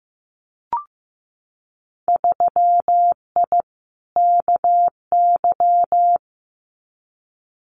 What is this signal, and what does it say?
Morse code, heard as a keyed tone, about 15 wpm, 700 Hz. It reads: 3I KY
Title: Winter Field Day Exchanges 15wpm